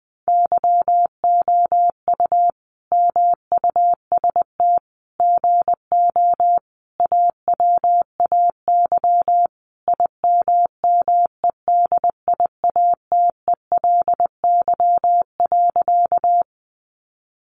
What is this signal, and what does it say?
Morse code, heard as a keyed tone, about 19 wpm, 700 Hz. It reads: YOU MUST GO AWAY IMMEDIATELY.
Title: Sentences from Top 1100 Words 20wpm